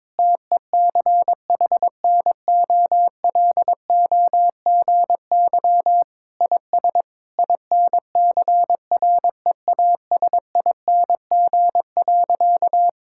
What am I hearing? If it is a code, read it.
Morse code, heard as a keyed tone, about 22 wpm, 700 Hz. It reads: TECHNOLOGY IS INCREASING.
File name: practice / 22wpm / Sentences with 2-Word Phrases - 22wpm